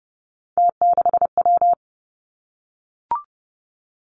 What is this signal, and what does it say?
Morse code, heard as a keyed tone, about 30 wpm, 700 Hz. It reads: T6W
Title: Three Challenging Letter-Number Combinations 30wpm